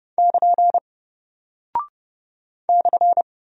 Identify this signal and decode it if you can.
Morse code, heard as a keyed tone, about 30 wpm, 700 Hz. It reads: <KN> /
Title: Single Character (Letter:Number:Punctuation:Prosign) Alt 30wpm